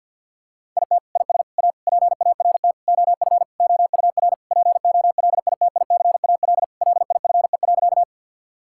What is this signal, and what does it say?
Morse code, heard as a keyed tone, about 50 wpm, 700 Hz. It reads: IT IS A PART OF OUR POLITICAL LIFE.
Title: Sentences from Top 1000 Words - Review Part 5 - 50wpm